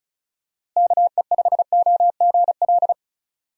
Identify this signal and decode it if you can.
Morse code, heard as a keyed tone, about 36 wpm, 700 Hz. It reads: KE5OGL
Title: USA Call Signs - 6 Characters - Repeated 2x - 35wpm